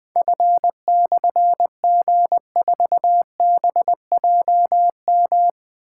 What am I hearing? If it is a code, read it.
Morse code, heard as a keyed tone, about 21 wpm, 700 Hz. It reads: F/G4BJM
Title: Difficult Call Signs 20wpm